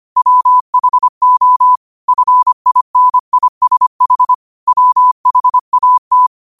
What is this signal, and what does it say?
Morse code, heard as a keyed tone, about 25 wpm, 1000 Hz. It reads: WHO FINISH WHAT